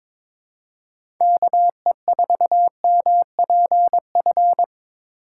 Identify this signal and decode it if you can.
Morse code, heard as a keyed tone, about 22 wpm, 700 Hz. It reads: KE4MPF
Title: USA Call Signs - 3x Character Spacing - 22wpm